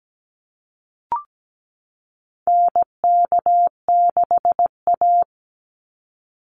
Morse code, heard as a keyed tone, about 17 wpm, 700 Hz. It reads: NK6A
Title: NA SOTA Chaser Callsigns 17wpm